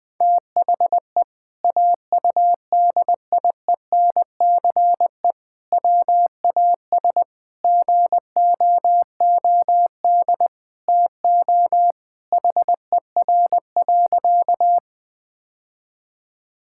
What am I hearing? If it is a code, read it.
Morse code, heard as a keyed tone, about 21 wpm, 700 Hz. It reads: THE AUDIENCE WAS GOOD TO HER.
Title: Sentences from Top 1000 Words - Review Part 5 - 20wpm